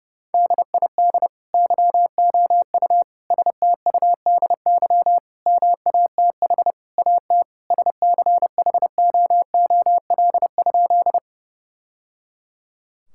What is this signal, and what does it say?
Morse code, heard as a keyed tone, about 30 wpm, 700 Hz. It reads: DID YOU STUDY MATH AT SCHOOL?
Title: Sentences from Top 1000 Words - Review Part 6 - 30wpm